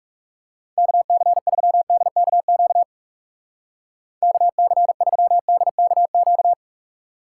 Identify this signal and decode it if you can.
Morse code, heard as a keyed tone, about 38 wpm, 700 Hz. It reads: KC3DKQ KC3DKQ